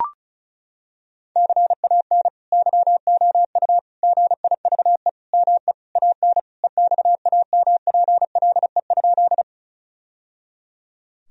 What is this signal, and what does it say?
Morse code, heard as a keyed tone, about 34 wpm, 700 Hz. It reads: CAN YOU GIVE ME AN EXAMPLE?
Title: Sentences from Top 1000 Words - Review Part 1 - 35wpm